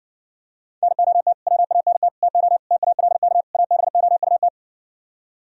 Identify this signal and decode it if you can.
Morse code, heard as a keyed tone, about 65 wpm, 700 Hz. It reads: NOT WANT TO TALK ABOUT